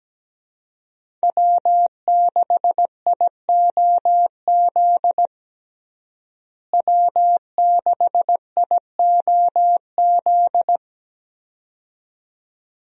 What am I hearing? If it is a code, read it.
Morse code, heard as a keyed tone, about 17 wpm, 700 Hz. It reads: W6IOZ W6IOZ
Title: USA Call Signs - 5 Characters - Repeated 2x - 17wpm